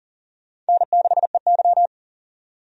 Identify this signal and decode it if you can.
Morse code, heard as a keyed tone, about 40 wpm, 700 Hz. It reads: N6EY
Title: USA Call Signs - 4 Characters - 40wpm